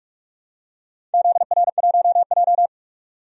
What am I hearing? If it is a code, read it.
Morse code, heard as a keyed tone, about 46 wpm, 700 Hz. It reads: ZR1J
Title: DXCC-100 Call Signs - 4 Characters - 45wpm